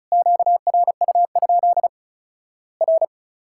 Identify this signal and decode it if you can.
Morse code, heard as a keyed tone, about 36 wpm, 700 Hz. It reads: QRU? R
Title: Worked All Europe DX Contest - QTC Exchange - 35wpm